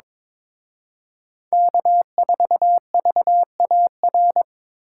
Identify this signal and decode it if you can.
Morse code, heard as a keyed tone, about 26 wpm, 700 Hz. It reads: K4VAR
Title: Speed Racing - USA 1x3 Call Signs 22wpm